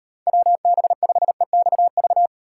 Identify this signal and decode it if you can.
Morse code, heard as a keyed tone, about 38 wpm, 700 Hz. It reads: WB5EXV